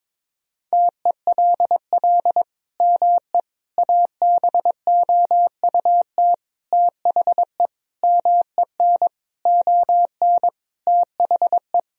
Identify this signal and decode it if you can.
Morse code, heard as a keyed tone, about 22 wpm, 700 Hz. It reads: TELL ME ABOUT THE MEN ON THE
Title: Sentences from Top 1000 Words - Review Part 2 - 22wpm